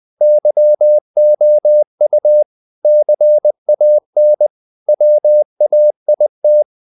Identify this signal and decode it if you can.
Morse code, heard as a keyed tone, about 20 wpm, 600 Hz. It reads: YOU CAN WAIT
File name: CW / Practice / If20